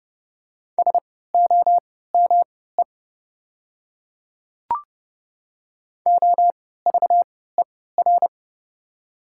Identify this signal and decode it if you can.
Morse code, heard as a keyed tone, about 30 wpm, spaced as for 17 wpm, 700 Hz. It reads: SOME OVER
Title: 3x Character Spacing - Top 100 Words - 30wpm